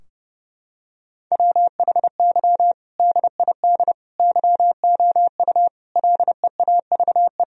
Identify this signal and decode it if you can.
Morse code, heard as a keyed tone, about 30 wpm, 700 Hz. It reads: WHY DID YOU LEAVE
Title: Sentences from Top 1000 Words - Review Part 3 - 30wpm